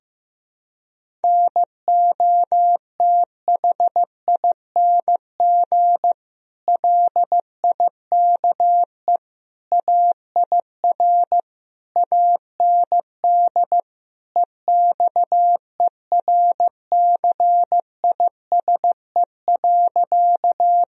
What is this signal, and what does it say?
Morse code, heard as a keyed tone, about 15 wpm, 700 Hz. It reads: NOTHING LIKE AIR AND EXERCISE.
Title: Sentences from Top 1000 Words - Review Part 4 - 15wpm